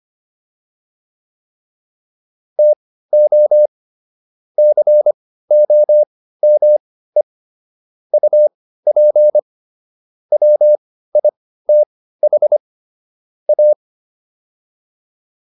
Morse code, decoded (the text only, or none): TO COME UP WITH A